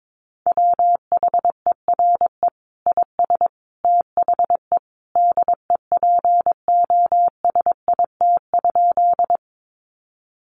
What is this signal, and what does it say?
Morse code, heard as a keyed tone, about 22 wpm, 700 Hz. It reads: WHERE IS THE DEPOSIT?
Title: Sentences from Top 1300 Words 22wpm